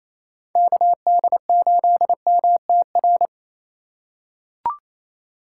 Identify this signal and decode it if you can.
Morse code, heard as a keyed tone, about 28 wpm, 700 Hz. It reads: KD8MTR
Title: USA Call Signs - 3x Character Spacing - 28wpm